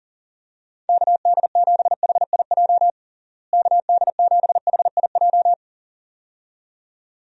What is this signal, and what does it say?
Morse code, heard as a keyed tone, about 37 wpm, 700 Hz. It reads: KD7HIJ KD7HIJ
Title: USA Call Signs - 6 Characters - Repeated 2x - 40wpm